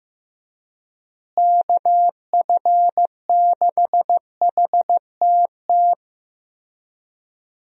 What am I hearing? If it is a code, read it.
Morse code, heard as a keyed tone, about 15 wpm, 700 Hz. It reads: KF6HTT
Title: USA Call Signs - 6 Characters - Repeated 3x - 15wpm